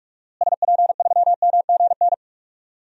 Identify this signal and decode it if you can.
Morse code, heard as a keyed tone, about 46 wpm, 700 Hz. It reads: SP3MGN
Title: DXCC-100 Call Signs - 6 Characters - Repeated 3x - 45wpm